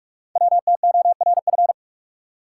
Morse code, heard as a keyed tone, about 44 wpm, 700 Hz. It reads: WTORF
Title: Code Groups - 5 Letter:Numbers - 45wpm